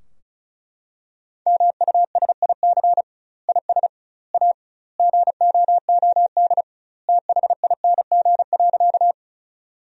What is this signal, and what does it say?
Morse code, heard as a keyed tone, about 34 wpm, 700 Hz. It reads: MUSIC IS A GOOD THING.
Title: 2x Word-Spacing - Sentences from Top 300 Words - 35wpm